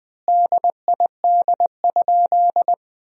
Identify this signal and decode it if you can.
Morse code, heard as a keyed tone, about 20 wpm, 700 Hz. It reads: DID?
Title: 3x Word-Spacing - Sentences from Top 100 Words - 20wpm